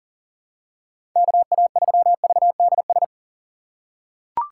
Morse code, heard as a keyed tone, about 38 wpm, 700 Hz. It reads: KA3VDS
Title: USA Call Signs - 6 Characters - 3x Character Spacing - 40wpm